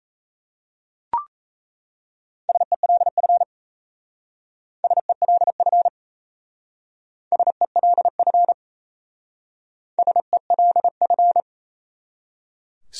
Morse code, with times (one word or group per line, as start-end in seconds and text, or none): 2.5-3.4 SELF
4.8-5.9 SELF
7.3-8.5 SELF
10.0-11.4 SELF